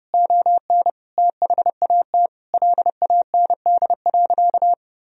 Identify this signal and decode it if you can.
Morse code, heard as a keyed tone, about 32 wpm, 700 Hz. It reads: ON THAT LAND.
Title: Sentences from Top 1000 Words - Review Part 1 - 30wpm